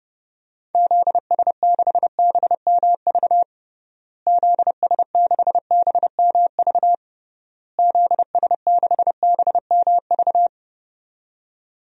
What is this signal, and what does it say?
Morse code, heard as a keyed tone, about 30 wpm, 700 Hz. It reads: ZS6BMV ZS6BMV ZS6BMV
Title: DXCC-100 Call Signs - 6 Characters - Repeated 3x - 30wpm